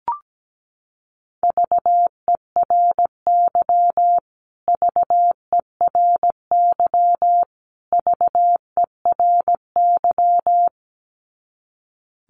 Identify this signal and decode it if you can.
Morse code, heard as a keyed tone, about 17 wpm, 700 Hz. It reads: VERY VERY VERY
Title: Top 200 Words Repeated 17wpm